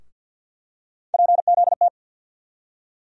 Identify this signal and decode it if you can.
Morse code, heard as a keyed tone, about 51 wpm, 700 Hz. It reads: PZT
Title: USA Call Sign Postfixes - 3 Letters 50wpm